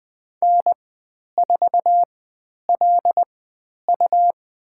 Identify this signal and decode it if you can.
Morse code, heard as a keyed tone, about 20 wpm, spaced as for 10 wpm, 700 Hz. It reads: N4LU
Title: 109 Lesson 18g - Letter m - Call Signs 20wpm